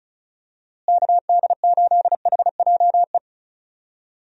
Morse code, heard as a keyed tone, about 34 wpm, 700 Hz. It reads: KD8HJE
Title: USA Call Signs - 6 Characters - Repeated 2x - 35wpm